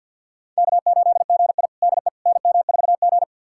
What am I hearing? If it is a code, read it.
Morse code, heard as a keyed tone, about 50 wpm, 700 Hz. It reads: K8GI DE NM4G
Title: USA Callsign DE Callsign - 4 Characters - 50wpm